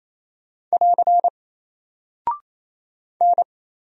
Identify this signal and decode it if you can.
Morse code, heard as a keyed tone, about 28 wpm, 700 Hz. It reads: <AR> N
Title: Single Character (Letter:Number:Punctuation:Prosign) Alt 28wpm